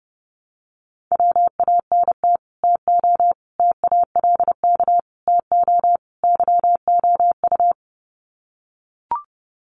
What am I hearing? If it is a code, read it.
Morse code, heard as a keyed tone, about 30 wpm, 700 Hz. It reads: WANT TO TALK TO YOU
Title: Sets of 5 Words Encore 30wpm